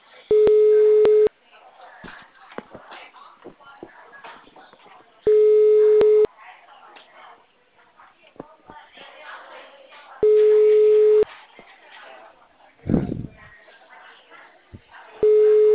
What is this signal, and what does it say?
Ringback tone (Europe, 425 Hz), with speech.